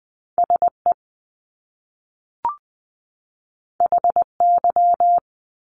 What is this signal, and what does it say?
Morse code, heard as a keyed tone, about 20 wpm, 700 Hz. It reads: SE HY
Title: Top Two Letter Combinations in Words 20wpm